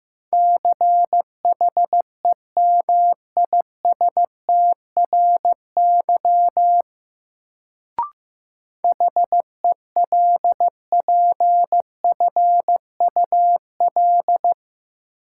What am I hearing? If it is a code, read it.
Morse code, heard as a keyed tone, about 15 wpm, 700 Hz. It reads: CHEMISTRY HELPFUL
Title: Sentences from Top 1000 Words - Review Part 6 - New Word Drill - 15wpm